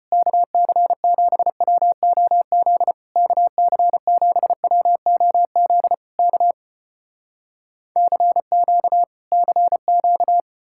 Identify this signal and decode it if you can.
Morse code, heard as a keyed tone, about 34 wpm, 700 Hz. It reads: KC7WOZ KC7WOZ K CQ CQ